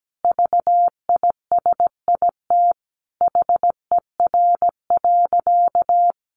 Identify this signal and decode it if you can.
Morse code, heard as a keyed tone, about 17 wpm, 700 Hz. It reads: VISIT HER.
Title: Easy Phrases 17wpm